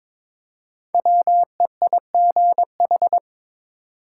Morse code, heard as a keyed tone, about 22 wpm, 700 Hz. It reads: WEIGH
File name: 5 Letter Words - First 3 Letters Given - 22wpm